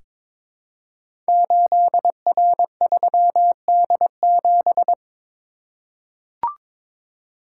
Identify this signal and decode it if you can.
Morse code, heard as a keyed tone, about 22 wpm, 700 Hz. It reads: 8R3D7
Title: Code Groups - 5 Letter:Numbers - 22wpm